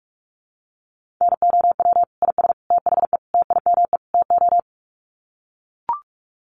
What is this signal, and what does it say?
Morse code, heard as a keyed tone, about 44 wpm, 700 Hz. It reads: NOW IS THE TIME TO
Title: Sets of 5 Words Encore2 45wpm